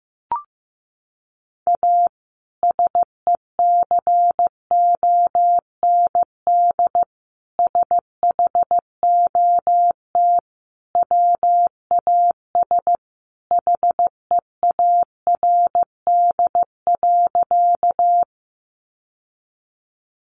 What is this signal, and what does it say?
Morse code, heard as a keyed tone, about 15 wpm, 700 Hz. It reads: A SECOND SHOT WAS HEARD.
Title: Sentences from Top 1100 Words 15wpm